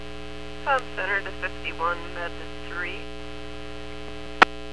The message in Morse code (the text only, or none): none